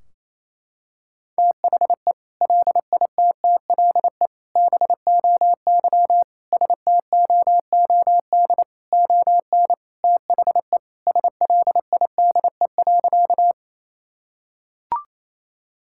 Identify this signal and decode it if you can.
Morse code, heard as a keyed tone, about 28 wpm, 700 Hz. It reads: THE LITTLE BOY STOOD ON THE SLIDE.
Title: Sentences from Top 1600 Words 28wpm